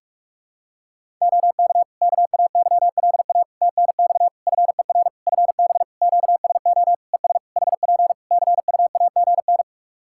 Morse code, heard as a keyed tone, about 46 wpm, 700 Hz. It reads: OK KAYLA TNX FER FB QSO ES HP CUAGN
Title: Practice QSO - Exchange 4 - Ending - 45wpm